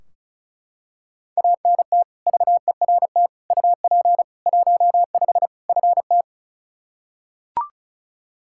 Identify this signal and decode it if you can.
Morse code, heard as a keyed tone, about 34 wpm, 700 Hz. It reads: ANT VERT UP 15 FT
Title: K1USN SST QSO Practice 35wpm